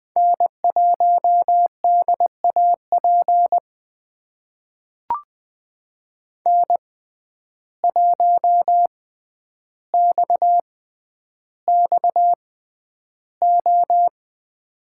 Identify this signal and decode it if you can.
Morse code, heard as a keyed tone, about 20 wpm, spaced as for 10 wpm, 700 Hz. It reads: N1DAP N1XXO
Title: USA Call Signs - 5 Characters - 6x Character Spacing - 20wpm